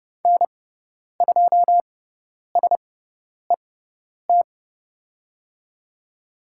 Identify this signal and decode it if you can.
Morse code, heard as a keyed tone, about 30 wpm, spaced as for 10 wpm, 700 Hz. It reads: N2SET
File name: Lesson 17g - Letter u - Call Signs 30wpm